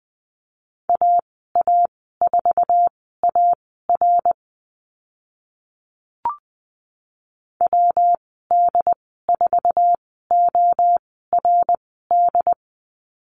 Morse code, heard as a keyed tone, about 20 wpm, spaced as for 14 wpm, 700 Hz. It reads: AA4AR WD4ORD
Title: 122 Lesson 20g - Number 3 - Call Signs 20wpm